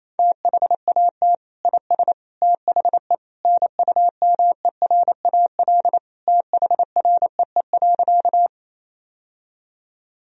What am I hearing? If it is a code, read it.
Morse code, heard as a keyed tone, about 28 wpm, 700 Hz. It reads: THAT IS THE NUMERAL THREE.
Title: Sentences from Top 1000 Words - Review Part 2 - 28wpm